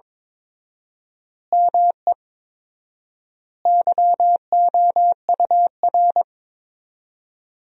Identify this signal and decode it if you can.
Morse code, heard as a keyed tone, about 22 wpm, 700 Hz. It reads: ME YOUR
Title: 4x Word-Spacing - Sentences from Top 200 Words - 22wpm